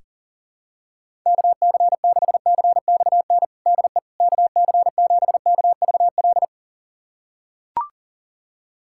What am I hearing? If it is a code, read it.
Morse code, heard as a keyed tone, about 40 wpm, 700 Hz. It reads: KC6CXN DE KC7KVL
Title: USA Callsign DE Callsign - 6 Characters - 40wpm